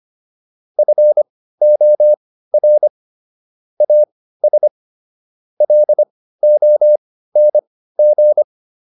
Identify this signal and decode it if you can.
Morse code, heard as a keyed tone, about 25 wpm, spaced as for 15 wpm, 600 Hz. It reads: FOR AS LONG